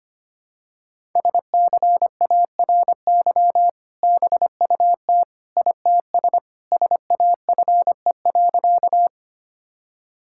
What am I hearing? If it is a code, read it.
Morse code, heard as a keyed tone, about 25 wpm, 700 Hz. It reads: SCARY BUT ITS SAFE.
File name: practice / 25wpm / Sentences with 2-Word Phrases - 25wpm